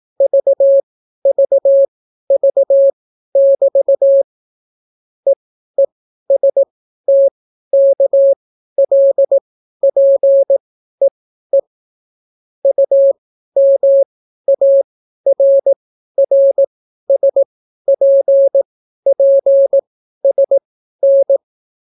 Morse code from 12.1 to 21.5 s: UMARRSPPSN